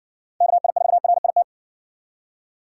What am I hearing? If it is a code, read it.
Morse code, heard as a keyed tone, about 56 wpm, 700 Hz. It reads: KI4RIT